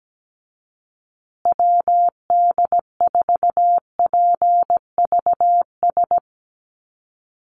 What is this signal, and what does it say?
Morse code, heard as a keyed tone, about 17 wpm, 700 Hz. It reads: WD4PVS
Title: USA Call Signs Repeated 2x 17wpm